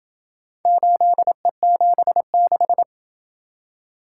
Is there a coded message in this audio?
Yes: Morse code from 0.7 to 2.8 s, heard as a keyed tone, about 27 wpm, 700 Hz. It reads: 8E76